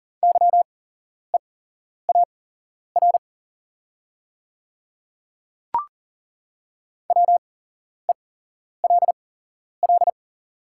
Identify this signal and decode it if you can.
Morse code, heard as a keyed tone, about 40 wpm, spaced as for 11 wpm, 700 Hz. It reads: YEAR WELL
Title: 8x Character Spacing - Top 300 Words - 40wpm